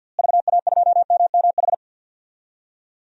Morse code, heard as a keyed tone, about 50 wpm, 700 Hz. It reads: VA2MMH
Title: DXCC-100 Call Signs - 6 Characters - Repeated 2x - 50wpm